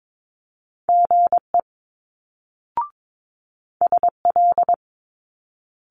Morse code, heard as a keyed tone, about 22 wpm, 700 Hz. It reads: GE SL
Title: Multi-letter Phonemes Quick Fire 22wpm